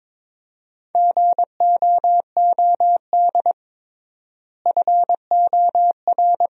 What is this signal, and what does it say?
Morse code, heard as a keyed tone, about 22 wpm, 700 Hz. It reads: GOOD FOR
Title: 3x Word-Spacing - Sentences from Top 300 Words - 22wpm